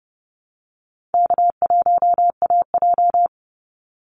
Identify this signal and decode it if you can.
Morse code, heard as a keyed tone, about 30 wpm, 700 Hz. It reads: K1AJ
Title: Call Signs Repeated 3x - 30wpm